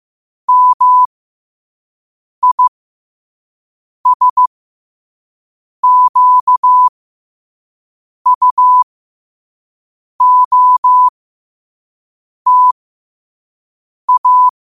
Morse code, heard as a keyed tone, about 15 wpm, spaced as for 5 wpm, 1000 Hz. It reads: MISQUOTA
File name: qotdfiles / 5wpm